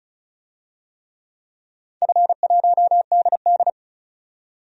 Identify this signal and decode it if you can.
Morse code, heard as a keyed tone, about 36 wpm, 700 Hz. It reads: F1DD
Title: DXCC-100 Call Signs - 4 Characters - 35wpm